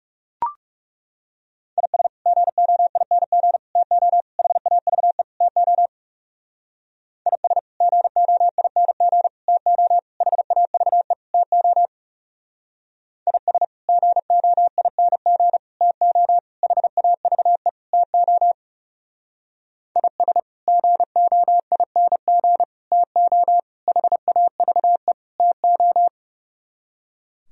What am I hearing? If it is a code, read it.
Morse code, heard as a keyed tone, about 36 wpm, 700 Hz. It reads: IS GOING TO HAVE TO IS GOING TO HAVE TO IS GOING TO HAVE TO IS GOING TO HAVE TO
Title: Speed-Racing - Sets of 5 Words 30wpm